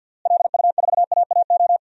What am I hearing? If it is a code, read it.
Morse code, heard as a keyed tone, about 51 wpm, 700 Hz. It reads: LU4AAO
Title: DXCC-100 Call Signs - 6 Characters - Repeated 3x - 50wpm